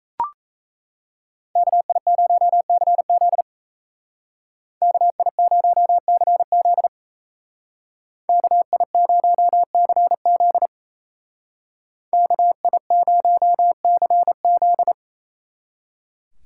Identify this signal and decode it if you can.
Morse code, heard as a keyed tone, about 34 wpm, 700 Hz. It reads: KI0CZ KI0CZ KI0CZ KI0CZ
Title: Speed Racing - USA 2x2 Call Signs 28wpm